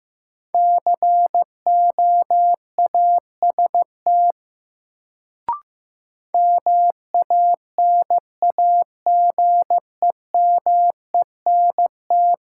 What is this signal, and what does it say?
Morse code, heard as a keyed tone, about 15 wpm, 700 Hz. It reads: COAST MANAGEMENT
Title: Sentences from Top 1000 Words - Review Part 4 - New Word Drill - 15wpm